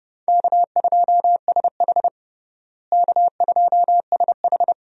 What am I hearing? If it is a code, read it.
Morse code, heard as a keyed tone, about 30 wpm, 700 Hz. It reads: K2SH K2SH
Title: USA Call Signs - 4 Characters - Repeated 3x - 30wpm